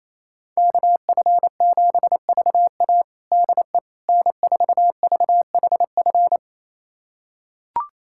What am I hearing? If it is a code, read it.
Morse code, heard as a keyed tone, about 28 wpm, 700 Hz. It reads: KF7VA DE N4VHF
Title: USA Callsign DE Callsign - 5 Characters - 28wpm